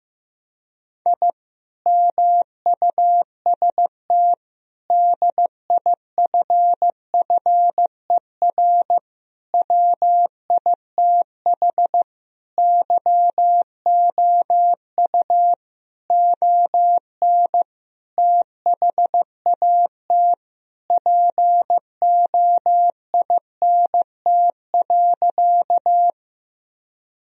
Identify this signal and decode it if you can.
Morse code, heard as a keyed tone, about 15 wpm, 700 Hz. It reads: I MUST DIFFER WITH YOU ON THAT POINT.
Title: Sentences from Top 1000 Words - Review Part 1 - 15wpm